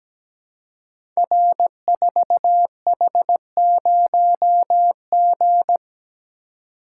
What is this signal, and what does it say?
Morse code, heard as a keyed tone, about 17 wpm, 700 Hz. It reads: R4H0G
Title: Code Groups - 5 Letter:Numbers - 17wpm